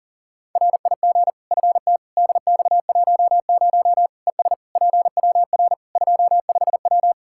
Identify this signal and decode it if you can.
Morse code, heard as a keyed tone, about 40 wpm, 700 Hz. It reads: RIG FT DX10 ES PWR 25W
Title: Practice QSO - Exchange 2 - Info - 40wpm